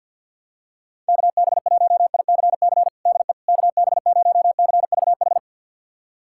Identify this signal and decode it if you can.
Morse code, heard as a keyed tone, about 46 wpm, 700 Hz. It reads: KB1ICC DE KB0CVH
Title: USA Callsign DE Callsign 50wpm